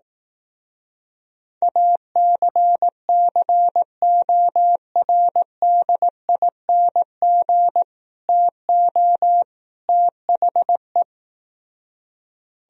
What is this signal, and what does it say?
Morse code, heard as a keyed tone, about 18 wpm, 700 Hz. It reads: ACCORDING TO THE
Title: Speed-Racing - Sets of 3 Words 15wpm